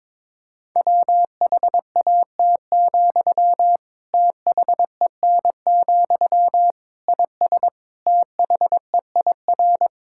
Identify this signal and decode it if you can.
Morse code, heard as a keyed tone, about 22 wpm, 700 Hz. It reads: WHAT, THEN, IS THEIR
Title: Sentences from Top 1000 Words - Review Part 6 - 22wpm